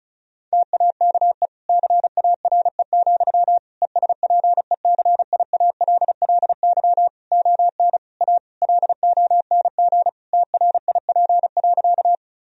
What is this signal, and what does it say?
Morse code, heard as a keyed tone, about 36 wpm, 700 Hz. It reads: TAKE CARE, ESPECIALLY ON A LONG TRIP.
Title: Sentences from Top 1000 Words - Review Part 4 - 35wpm